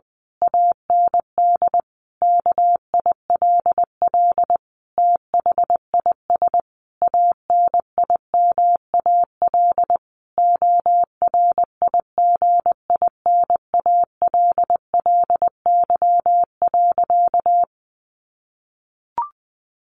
Morse code, heard as a keyed tone, about 20 wpm, 700 Hz. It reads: AND KILL THIS ANIMAL ORIGINALLY.
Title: A Sound Of Thunder 20wpm